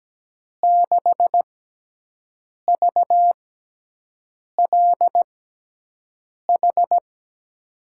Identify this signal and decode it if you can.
Morse code, heard as a keyed tone, about 17 wpm, 700 Hz. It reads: 6 V L H
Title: USA Call Signs - 6 Characters - 6x Character Spacing - 17wpm